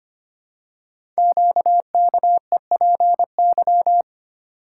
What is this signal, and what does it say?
Morse code, heard as a keyed tone, about 25 wpm, 700 Hz. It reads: QKEPY
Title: Code Groups - 5 Letters - 25wpm